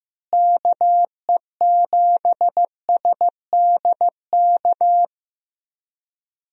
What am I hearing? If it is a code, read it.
Morse code, heard as a keyed tone, about 15 wpm, 700 Hz. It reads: KE7SDK
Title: USA Call Signs - 6 Characters - Repeated 2x - 15wpm